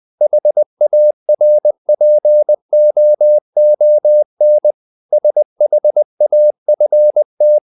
Morse code, heard as a keyed tone, about 20 wpm, 600 Hz. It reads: HARPOON SHAFT
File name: the old man and the sea part 2 20wpm